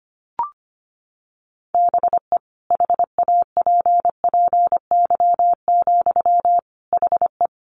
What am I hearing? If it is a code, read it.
Morse code, heard as a keyed tone, about 25 wpm, 700 Hz. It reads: BE HAPPY, HE
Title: Sentences from Top 1200 Words 25wpm